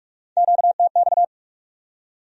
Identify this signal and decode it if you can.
Morse code, heard as a keyed tone, about 59 wpm, 700 Hz. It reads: QTX